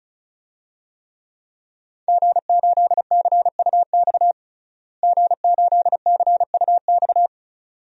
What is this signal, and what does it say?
Morse code, heard as a keyed tone, about 34 wpm, 700 Hz. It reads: G8CUX G8CUX
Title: DXCC-100 Call Signs - 5 Characters - Repeated 2x - 35wpm